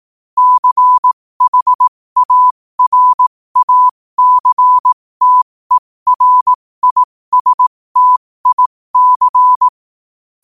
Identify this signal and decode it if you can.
Morse code, heard as a keyed tone, about 18 wpm, spaced as for 15 wpm, 1000 Hz. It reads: CHARACTERISTIC
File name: qotdfiles / 15wpm